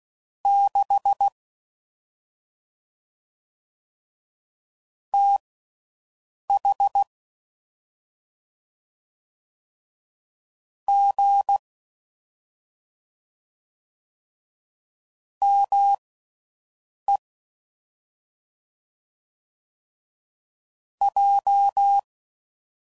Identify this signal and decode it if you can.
Morse code, heard as a keyed tone, about 16 wpm, spaced as for 6 wpm, 800 Hz. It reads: 6 TH G ME J
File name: Mixed / CW Comparisons / Comparisons Broken Characters